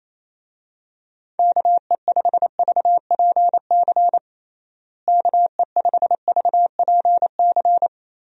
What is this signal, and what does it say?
Morse code, heard as a keyed tone, about 27 wpm, 700 Hz. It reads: KE5VPC KE5VPC
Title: USA Call Signs - 6 Characters - Repeated 3x - 28wpm